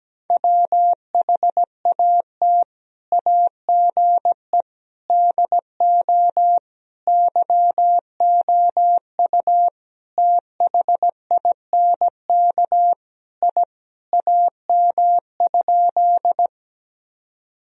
Morse code, heard as a keyed tone, about 17 wpm, 700 Hz. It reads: WHAT AGE DO YOU THINK I AM?
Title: Sentences from Top 1000 Words - Review Part 3 - 17wpm